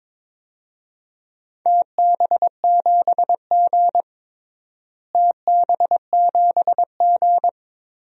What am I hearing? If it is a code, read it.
Morse code, heard as a keyed tone, about 22 wpm, 700 Hz. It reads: TB7G TB7G
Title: DXCC-100 Call Signs - 4 Characters - Repeated 2x - 22wpm